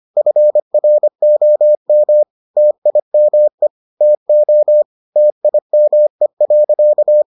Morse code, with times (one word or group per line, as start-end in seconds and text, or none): 0.2-2.2 FROM
2.6-3.7 TIME
4.0-4.8 TO
5.2-7.2 TIME.